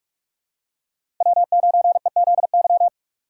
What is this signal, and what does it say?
Morse code, heard as a keyed tone, about 43 wpm, 700 Hz. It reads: W9EZY
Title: USA Call Signs - 5 Characters - 3x Character Spacing - 45wpm